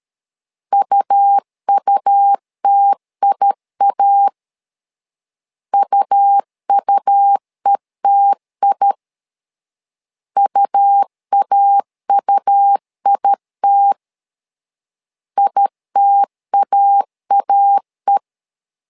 Morse code, heard as a keyed tone, about 13 wpm, 800 Hz. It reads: UUTIA UUETI UAUIT ITAAE
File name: LEZIONI / 2 UHDN / 2 U 12wpm 1338